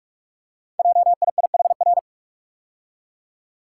Morse code, heard as a keyed tone, about 44 wpm, 690 Hz. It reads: JIIHR
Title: Code Groups - 5 Letters - 45wpm